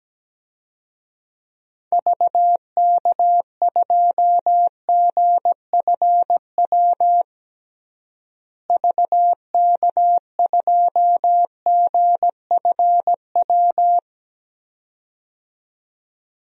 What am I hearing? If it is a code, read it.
Morse code, heard as a keyed tone, about 17 wpm, 700 Hz. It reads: VK2GFW VK2GFW
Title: DXCC-100 Call Signs - 6 Characters - Repeated 2x - 17wpm